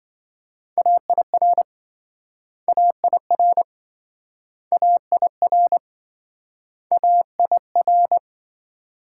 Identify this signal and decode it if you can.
Morse code, heard as a keyed tone, about 26 wpm, 700 Hz. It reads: AIR AIR AIR AIR